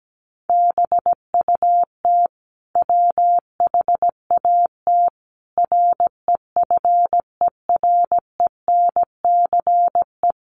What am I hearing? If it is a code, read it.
Morse code, heard as a keyed tone, about 17 wpm, 700 Hz. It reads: BUT WHAT REFERENCE
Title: Sentences from Top 1200 Words 17wpm